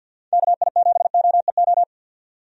Morse code, heard as a keyed tone, about 53 wpm, 700 Hz. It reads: KI7OEO
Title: USA Call Signs - 6 Characters - 50wpm